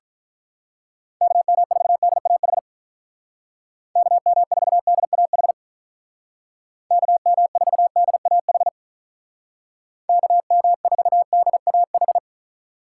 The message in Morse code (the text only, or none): KM4DAH KM4DAH KM4DAH KM4DAH